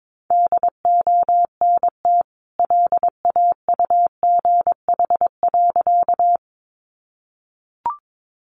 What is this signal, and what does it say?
Morse code, heard as a keyed tone, about 22 wpm, 700 Hz. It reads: DONT LAUGH.
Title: Sentences from Top 500 Words 22wpm